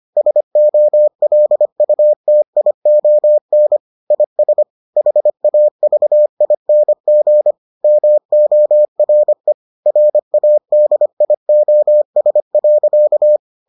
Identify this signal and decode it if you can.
Morse code, heard as a keyed tone, about 25 wpm, 600 Hz. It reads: SOLUTION IS HAVING MORE RADIOS.